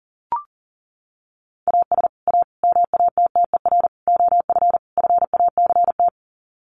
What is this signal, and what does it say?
Morse code, heard as a keyed tone, about 40 wpm, 700 Hz. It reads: AS A MATTER OF FACT
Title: Sets of 5 Words 40wpm